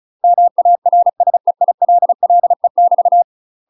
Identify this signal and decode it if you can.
Morse code, heard as a keyed tone, about 35 wpm, 700 Hz. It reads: MARSEILLE <BT>
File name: UK MASTER QSOs TRAINING / 35WPM / UK MASTER QSOs 0032